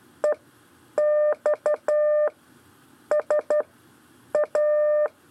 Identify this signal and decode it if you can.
Morse code, heard as a keyed tone, about 12 wpm, spaced as for 8 wpm, 600 Hz. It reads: EXSA
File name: 4 LEZIONE CW 1